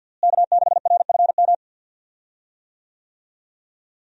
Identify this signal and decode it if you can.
Morse code, heard as a keyed tone, about 49 wpm, 700 Hz. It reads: K6RFM
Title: USA Call Signs - 5 Characters - Repeated 2x - 50wpm